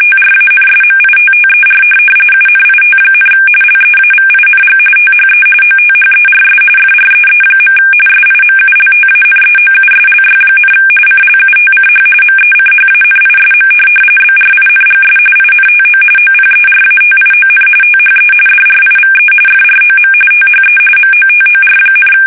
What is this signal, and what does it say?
RTTY, about 45.45 baud. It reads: TRANSMITTER UNIT, LEAVING THE OTHERS FREE. SIMULTANEOUSLY, CLUTCH MECHANISM BRINGS INTO OPERATION THE DRIVE FROM THE MOTOR,